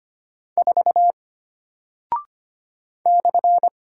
Morse code, heard as a keyed tone, about 25 wpm, 700 Hz. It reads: 4 /
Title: Single Character (Letter:Number:Punctuation:Prosign) Alt 25wpm